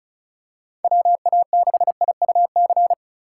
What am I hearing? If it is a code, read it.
Morse code, heard as a keyed tone, about 34 wpm, 700 Hz. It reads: WA6IUC
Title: USA Call Signs - 6 Characters - 3x Character Spacing - 35wpm